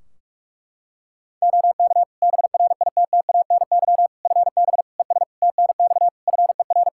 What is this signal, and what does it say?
Morse code, heard as a keyed tone, about 41 wpm, 700 Hz. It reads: OK BRITTANY FB ES TNX FER
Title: Practice QSO - Exchange 2 - Info - 45wpm